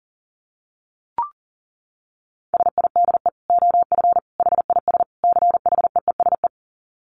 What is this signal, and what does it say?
Morse code, heard as a keyed tone, about 40 wpm, 700 Hz. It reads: SIDE OF HIS CHEESE
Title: Dit Challenge 40wpm